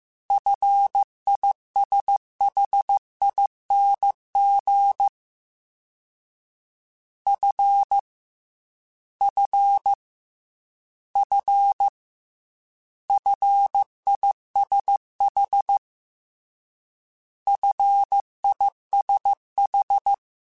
Morse code, heard as a keyed tone, about 15 wpm, 800 Hz. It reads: FISHING F F F FISH FISH
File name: Combinations / Combinations-mp3s / Combinations@13